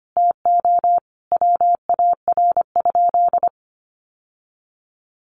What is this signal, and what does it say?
Morse code, heard as a keyed tone, about 25 wpm, 700 Hz. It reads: TO WAR?
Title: Sentences from Top 1400 Words 25wpm